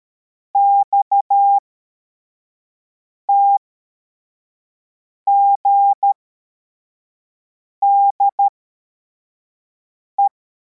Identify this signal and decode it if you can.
Morse code, heard as a keyed tone, about 13 wpm, spaced as for 4 wpm, 800 Hz. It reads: XTGDE
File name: formation / cw / Text33